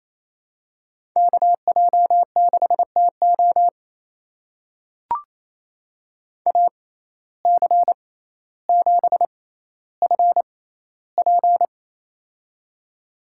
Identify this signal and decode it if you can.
Morse code, heard as a keyed tone, about 28 wpm, spaced as for 14 wpm, 700 Hz. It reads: KJ6TO AC7FP
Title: USA Call Signs - 5 Characters - 6x Character Spacing - 28wpm